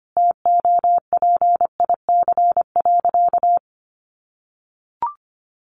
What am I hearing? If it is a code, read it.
Morse code, heard as a keyed tone, about 25 wpm, 700 Hz. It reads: TOPIC.
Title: Sentences from Top 1200 Words 25wpm